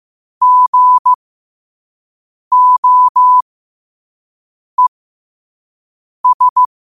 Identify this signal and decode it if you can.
Morse code, heard as a keyed tone, about 15 wpm, spaced as for 5 wpm, 1000 Hz. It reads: GOES
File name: qotdfiles / 5wpm